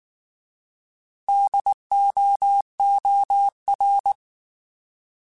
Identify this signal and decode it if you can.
Morse code, heard as a keyed tone, about 19 wpm, 800 Hz. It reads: DOOR
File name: Alphabetical / Alphabetical mp3s / Ds @18